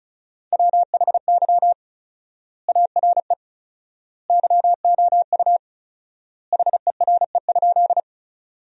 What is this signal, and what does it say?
Morse code, heard as a keyed tone, about 36 wpm, 700 Hz. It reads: WHY ARE YOU HERE?
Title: 4x Word-Spacing - Sentences from Top 200 Words - 35wpm